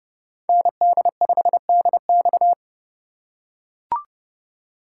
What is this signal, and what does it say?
Morse code, heard as a keyed tone, about 28 wpm, 700 Hz. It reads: ND5DX